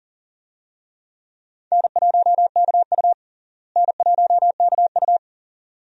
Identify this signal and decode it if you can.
Morse code, heard as a keyed tone, about 40 wpm, 700 Hz. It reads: N1KU N1KU
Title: USA Call Signs - 4 Characters - Repeated 2x - 40wpm